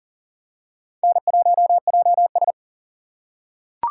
Morse code, heard as a keyed tone, about 40 wpm, 700 Hz. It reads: N1JS
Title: USA Call Signs - 4 Characters - 2x Character Spacing - 40wpm